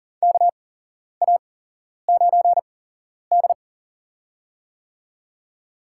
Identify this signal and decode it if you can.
Morse code, heard as a keyed tone, about 38 wpm, spaced as for 11 wpm, 700 Hz. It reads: KA9D
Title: USA Call Signs - 4 Characters - 8x Character Spacing - 40wpm